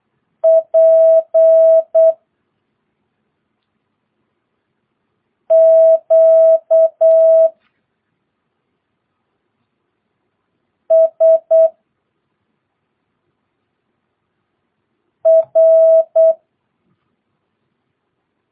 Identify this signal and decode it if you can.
Morse code, heard as a keyed tone, about 8 wpm, spaced as for 2 wpm, 650 Hz. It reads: PQSR
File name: jh8cbh / 2022cw